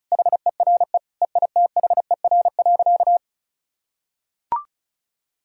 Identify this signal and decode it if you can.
Morse code, heard as a keyed tone, about 36 wpm, 700 Hz. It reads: HERE EITHER.
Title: Sentences from Top 1000 Words - Review Part 4 - 35wpm